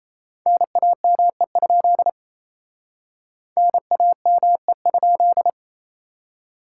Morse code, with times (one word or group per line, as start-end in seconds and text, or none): 0.5-2.1 NAME?
3.6-5.5 NAME?